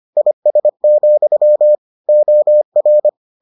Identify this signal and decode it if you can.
Morse code, heard as a keyed tone, about 25 wpm, 600 Hz. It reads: IS, OR